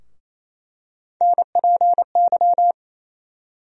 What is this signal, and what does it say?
Morse code, heard as a keyed tone, about 28 wpm, 700 Hz. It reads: NPY
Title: USA Call Sign Postfixes - 3 Letters 28wpm